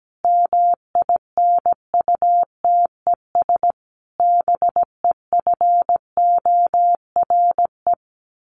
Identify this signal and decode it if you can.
Morse code, heard as a keyed tone, about 17 wpm, 700 Hz. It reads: MINUTES BEFORE
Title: A Sound Of Thunder 17wpm